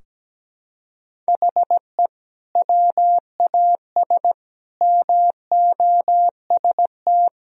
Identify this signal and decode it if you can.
Morse code, heard as a keyed tone, about 17 wpm, 700 Hz. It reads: HE WAS MOST
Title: Sentences from Top 1000 Words - Review Part 4 - 17wpm